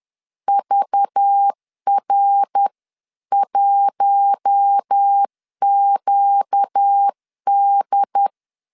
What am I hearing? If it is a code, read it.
Morse code, heard as a keyed tone, about 10 wpm, 800 Hz. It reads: VR1QD
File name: CW / DET / 1 6 1500C S C 10wpm 4252